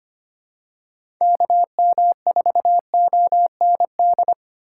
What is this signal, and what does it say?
Morse code, heard as a keyed tone, about 25 wpm, 700 Hz. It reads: KM4OND
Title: USA Call Signs - 4x Character Spacing - 25wpm